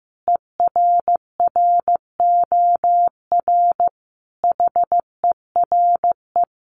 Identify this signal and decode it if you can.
Morse code, heard as a keyed tone, about 15 wpm, 700 Hz. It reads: ERROR HERE
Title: A Sound Of Thunder 15wpm